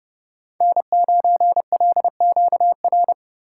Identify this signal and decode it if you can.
Morse code, heard as a keyed tone, about 30 wpm, 700 Hz. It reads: N9LQR
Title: USA Call Signs - 3x Character Spacing - 30wpm